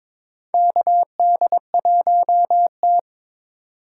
Morse code, heard as a keyed tone, about 22 wpm, 700 Hz. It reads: KD1T
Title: USA Call Signs - 4 Characters - 2x Character Spacing - 22wpm